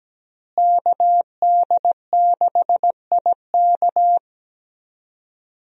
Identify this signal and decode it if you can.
Morse code, heard as a keyed tone, about 17 wpm, 700 Hz. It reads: KD6IK
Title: USA Call Signs - 5 Characters - Repeated 2x - 17wpm